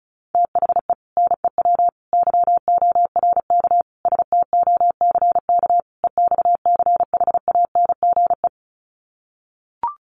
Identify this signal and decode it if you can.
Morse code, heard as a keyed tone, about 34 wpm, 700 Hz. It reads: THE NEW YORK STOCK EXCHANGE
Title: Sets of 5 Words 35wpm